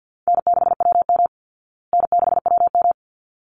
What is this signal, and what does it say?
Morse code, heard as a keyed tone, about 50 wpm, 700 Hz. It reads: N6WM N6WM
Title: Easy Call Signs Repeated 2x - 50wpm